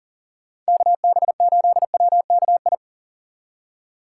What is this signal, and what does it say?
Morse code, heard as a keyed tone, about 39 wpm, 700 Hz. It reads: KB8WKI
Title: USA Call Signs - 3x Character Spacing - 40wpm